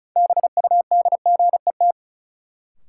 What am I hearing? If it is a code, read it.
Morse code, heard as a keyed tone, about 36 wpm, 700 Hz. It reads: BUDGET
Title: Sentences from Top 1200 Words - New Word Drill Alt - 35wpm